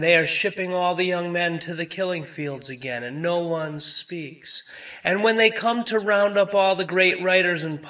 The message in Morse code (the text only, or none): none